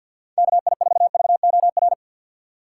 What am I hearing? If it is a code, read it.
Morse code, heard as a keyed tone, about 47 wpm, 700 Hz. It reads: KI4VOR
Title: USA Call Signs - 6 Characters - 3x Character Spacing - 50wpm